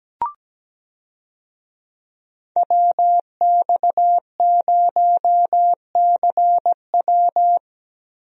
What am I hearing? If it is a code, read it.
Morse code, heard as a keyed tone, about 17 wpm, 700 Hz. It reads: WX0CW
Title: USA Call Signs - 5 Characters - Repeated 3x - 17wpm